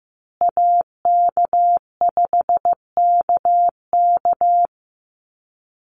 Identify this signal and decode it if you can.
Morse code, heard as a keyed tone, about 15 wpm, 700 Hz. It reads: AK5KK
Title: USA Call Signs Repeated 2x 15wpm